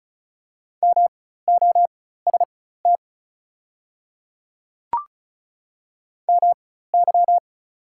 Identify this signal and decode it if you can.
Morse code, heard as a keyed tone, about 34 wpm, spaced as for 16 wpm, 700 Hz. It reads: MOST MY
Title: 4x Character Spacing - Top 100 Words - 35wpm